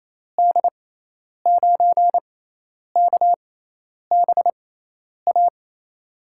Morse code, heard as a keyed tone, about 28 wpm, spaced as for 10 wpm, 700 Hz. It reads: D9KBA
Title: USA Call Signs - 6 Characters - 6x Character Spacing - 28wpm